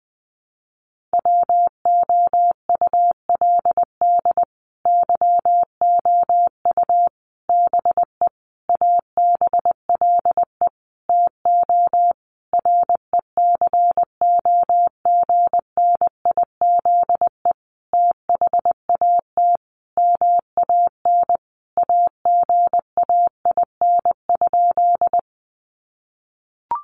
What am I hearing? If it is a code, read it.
Morse code, heard as a keyed tone, about 20 wpm, 700 Hz. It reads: WOULD YOU BE ABLE TO RECOGNIZE THAT MAN AGAIN?
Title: Sentences from Top 1400 Words 20wpm